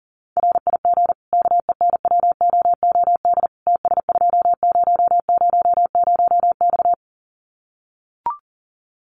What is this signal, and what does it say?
Morse code, heard as a keyed tone, about 40 wpm, 700 Hz. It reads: RIG KENWOOD TS2000X
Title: Common QSO Phrases 40wpm